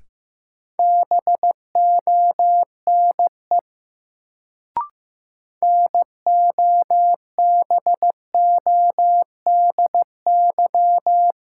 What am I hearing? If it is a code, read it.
Morse code, heard as a keyed tone, about 15 wpm, 700 Hz. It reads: BONE NOBODY
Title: Sentences from Top 1600 Words - New Word Drill - 15wpm